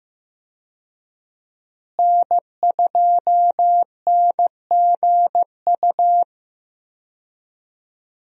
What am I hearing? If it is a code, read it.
Morse code, heard as a keyed tone, about 16 wpm, 700 Hz. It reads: N2NGU
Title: USA Call Signs - 5 Characters - Repeated 3x - 15wpm